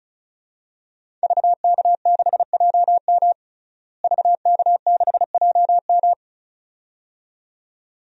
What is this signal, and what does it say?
Morse code, heard as a keyed tone, about 34 wpm, 700 Hz. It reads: VK6JM VK6JM
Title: DXCC-100 Call Signs - 5 Characters - Repeated 2x - 35wpm